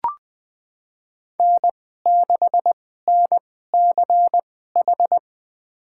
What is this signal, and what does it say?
Morse code, heard as a keyed tone, about 20 wpm, spaced as for 15 wpm, 700 Hz. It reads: N6NCH
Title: USA Call Signs - 5 Characters - 2x Character Spacing - 20wpm